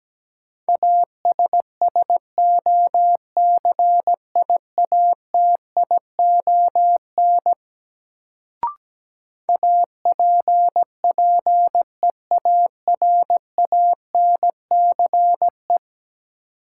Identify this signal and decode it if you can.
Morse code, heard as a keyed tone, about 17 wpm, 700 Hz. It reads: ASSOCIATION APPEARANCE
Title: Sentences from Top 1100 Words - New Word Drill - 17wpm